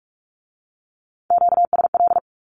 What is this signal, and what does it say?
Morse code, heard as a keyed tone, about 44 wpm, 700 Hz. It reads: QSL
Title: Q-Codes with Spoken Meaning - 45wpm